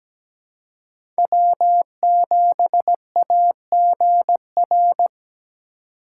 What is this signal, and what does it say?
Morse code, heard as a keyed tone, about 17 wpm, 700 Hz. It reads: W7AGR
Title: USA Call Signs - 5 Characters - 2x Character Spacing - 17wpm